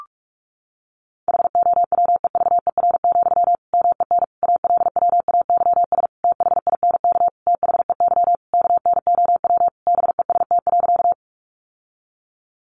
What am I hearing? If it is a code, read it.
Morse code, heard as a keyed tone, about 44 wpm, 700 Hz. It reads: HOWEVER, MEN ALWAYS THINK THEY KNOW BEST.
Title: Sentences from Top 600 Words 45wpm